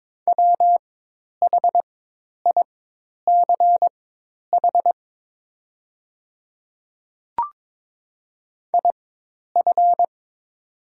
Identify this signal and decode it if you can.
Morse code, heard as a keyed tone, about 22 wpm, spaced as for 10 wpm, 700 Hz. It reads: WHICH IF
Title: 4x Character Spacing - Top 100 Words - 22wpm